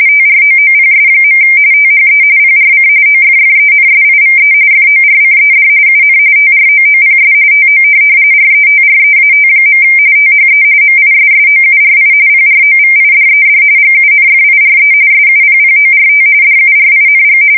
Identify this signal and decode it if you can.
RTTY, about 45.45 baud. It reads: TROOPS. THE PRC OFFENSIVE FORCED UN TROOPS TO RETREAT. THE MISSOURI PROVIDED GUNFIRE SUPPORT DURING
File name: fsk170 / USS-Missouri